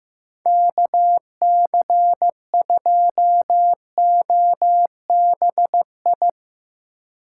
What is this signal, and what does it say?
Morse code, heard as a keyed tone, about 17 wpm, 700 Hz. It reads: KC2OBI
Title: Speed Racing - USA 2x3 Call Signs 15wpm